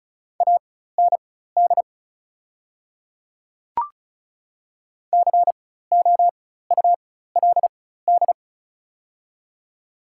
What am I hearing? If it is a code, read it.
Morse code, heard as a keyed tone, about 36 wpm, spaced as for 16 wpm, 690 Hz. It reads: AND COULD
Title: 4x Character Spacing - Top 100 Words - 35wpm